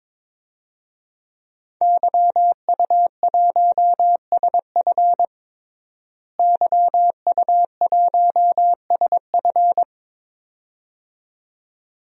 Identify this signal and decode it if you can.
Morse code, heard as a keyed tone, about 22 wpm, 700 Hz. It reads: YU1SF YU1SF
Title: DXCC-100 Call Signs - 5 Characters - Repeated 2x - 22wpm